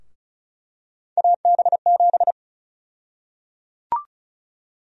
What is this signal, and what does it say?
Morse code, heard as a keyed tone, about 34 wpm, 700 Hz. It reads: AB7
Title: USA Call Sign Prefixes - Repeated 2x - 35wpm